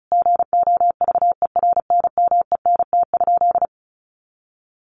Morse code, heard as a keyed tone, about 34 wpm, 700 Hz. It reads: GOVERNMENT?
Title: Sentences from Top 1400 Words 35wpm